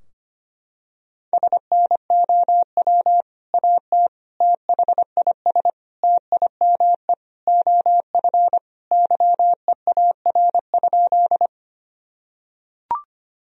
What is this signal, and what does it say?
Morse code, heard as a keyed tone, about 25 wpm, 700 Hz. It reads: SNOW AT THIS TIME OF YEAR?
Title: Sentences from Top 1000 Words - Review Part 3 - 25wpm